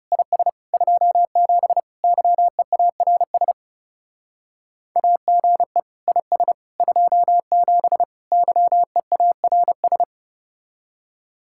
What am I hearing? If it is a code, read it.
Morse code, heard as a keyed tone, about 35 wpm, 700 Hz. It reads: IS 27 YEARS AGE IS 27 YEARS
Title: Speed-Racing Common QSO Phrases - 30wpm